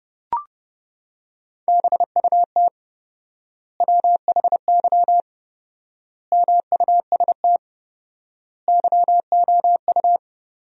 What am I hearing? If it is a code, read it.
Morse code, heard as a keyed tone, about 32 wpm, 700 Hz. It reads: BUT WHY MUST YOU
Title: 4x Word-Spacing - Sentences from Top 400 Words - 30wpm